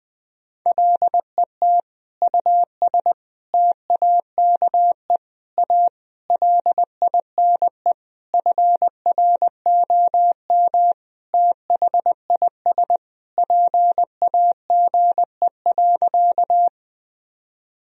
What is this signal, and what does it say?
Morse code, heard as a keyed tone, about 20 wpm, 700 Hz. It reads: LET US TAKE A LINE FROM THIS PAGE.
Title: Sentences from Top 1000 Words - Review Part 1 - 20wpm